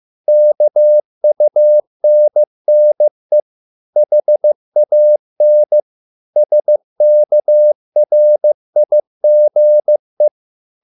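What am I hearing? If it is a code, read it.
Morse code, heard as a keyed tone, about 15 wpm, 600 Hz. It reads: KUNNE HAN SKRIGE
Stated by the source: Morse code at a slow 15 wpm from CW med Gnister.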